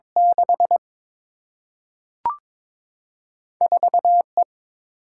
Morse code, heard as a keyed tone, about 22 wpm, 700 Hz. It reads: 6 4 E
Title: Single-Letter Single-Number 22wpm